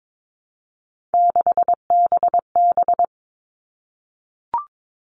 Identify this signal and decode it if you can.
Morse code, heard as a keyed tone, about 22 wpm, 700 Hz. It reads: 6BB
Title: All Cut Numbers - 3-Digit 22wpm